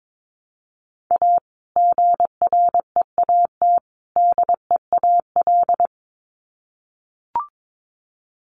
Morse code, heard as a keyed tone, about 22 wpm, 700 Hz. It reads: A GREAT DEAL
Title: Sets of 3 Words 22wpm